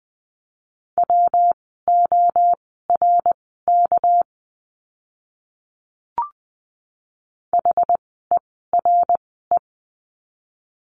Morse code, heard as a keyed tone, about 20 wpm, spaced as for 15 wpm, 700 Hz. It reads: WORK HERE
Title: 206 Lesson 33f - Letter j - Words 20wpm